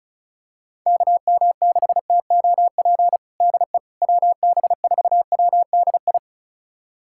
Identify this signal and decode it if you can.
Morse code, heard as a keyed tone, about 34 wpm, 700 Hz. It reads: KM6TOP DE WB4WDI
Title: USA Callsign DE Callsign - 6 Characters - 35wpm